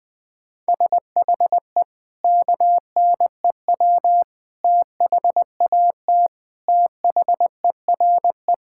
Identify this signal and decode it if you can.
Morse code, heard as a keyed tone, about 21 wpm, 700 Hz. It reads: SHE KNEW THAT THERE
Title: Sentences with 4-Word Phrases - 20wpm